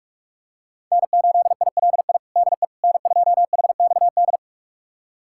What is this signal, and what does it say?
Morse code, heard as a keyed tone, about 41 wpm, 700 Hz. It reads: N8ILI DE N2HXD
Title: USA Callsign DE Callsign - 5 Characters - 45wpm